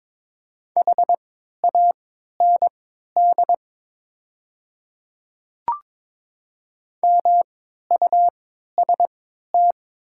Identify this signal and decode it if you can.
Morse code, heard as a keyed tone, about 22 wpm, spaced as for 12 wpm, 700 Hz. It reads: HAND MUST
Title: 3x Character Spacing - Top 200 Words - 22wpm